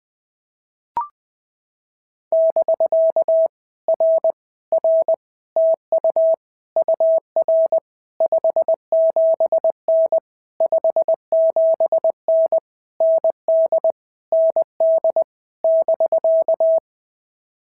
Morse code, heard as a keyed tone, about 21 wpm, 650 Hz. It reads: <BK> R R TU UR 57N 57N ND ND <BK>
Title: POTA QSO Practice - 20wpm